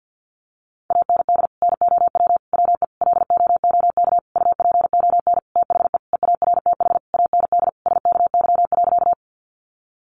Morse code, heard as a keyed tone, about 50 wpm, 700 Hz. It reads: AND NOW WE LOOK UPON THE EARTH AND SKY.
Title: Sentences from Top 1200 Words 50wpm